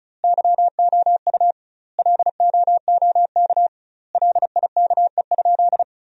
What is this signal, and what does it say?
Morse code, heard as a keyed tone, about 36 wpm, 700 Hz. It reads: YOU LOOK LIKE?
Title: 2x Word-Spacing - Sentences from Top 100 Words - 35wpm